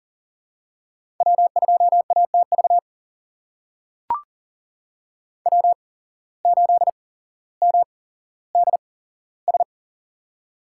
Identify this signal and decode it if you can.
Morse code, heard as a keyed tone, about 40 wpm, spaced as for 17 wpm, 700 Hz. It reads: W2ATV W8MDS
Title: USA Call Signs - 8x Character Spacing - 40wpm